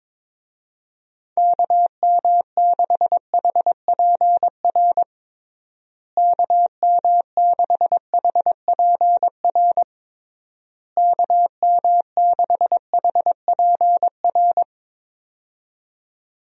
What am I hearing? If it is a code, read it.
Morse code, heard as a keyed tone, about 22 wpm, 700 Hz. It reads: KM6HPR KM6HPR KM6HPR
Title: USA Call Signs - 6 Characters - Repeated 3x - 22wpm